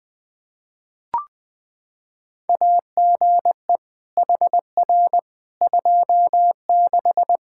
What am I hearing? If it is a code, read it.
Morse code, heard as a keyed tone, about 20 wpm, 700 Hz. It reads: AGE HR 26
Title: Practice QSO - Exchange 3 - More Info - 20wpm